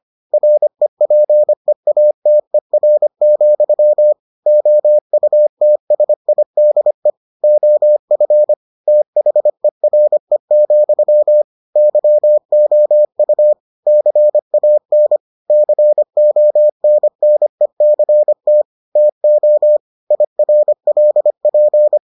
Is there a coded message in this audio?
Yes: Morse code from 0.3 to 22.0 s, heard as a keyed tone, about 25 wpm, 600 Hz. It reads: REPEATER, OUTSIDE OF THERE, YOU CAN CONNECT TO IRLP